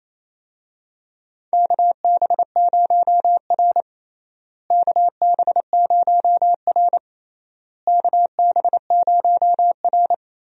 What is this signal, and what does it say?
Morse code, heard as a keyed tone, about 28 wpm, 700 Hz. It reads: KB0R KB0R KB0R
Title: USA Call Signs - 4 Characters - Repeated 3x - 28wpm